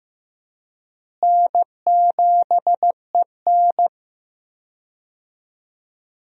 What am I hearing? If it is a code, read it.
Morse code, heard as a keyed tone, about 15 wpm, 700 Hz. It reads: N7EN
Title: USA Call Signs - 4 Characters - Repeated 3x - 15wpm